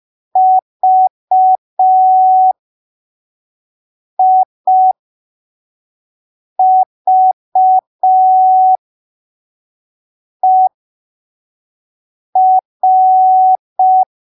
Morse code, heard as a keyed tone, about 5 wpm, spaced as for 3 wpm, 750 Hz. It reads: VIVER
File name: arquivos / texto / texto 21 05ppm